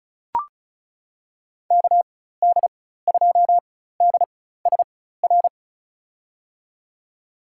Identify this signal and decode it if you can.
Morse code, heard as a keyed tone, about 34 wpm, spaced as for 16 wpm, 700 Hz. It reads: KD2DSR
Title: USA Call Signs - 6 Characters - 4x Character Spacing - 35wpm